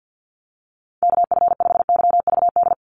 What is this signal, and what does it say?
Morse code, heard as a keyed tone, about 49 wpm, 700 Hz. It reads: KF5YVD
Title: USA Call Signs 50wpm